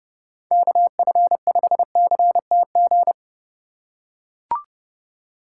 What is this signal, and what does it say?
Morse code, heard as a keyed tone, about 28 wpm, 700 Hz. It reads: KF5CTG